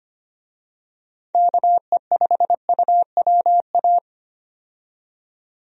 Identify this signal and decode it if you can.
Morse code, heard as a keyed tone, about 25 wpm, 700 Hz. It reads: KE5UWA
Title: USA Call Signs - 6 Characters - 25wpm